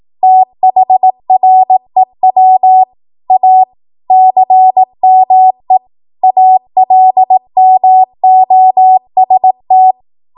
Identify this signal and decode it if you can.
Morse code, heard as a keyed tone, about 18 wpm, 750 Hz. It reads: THREW A CME ALMOST
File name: Archive / 18 WPM / 18WPM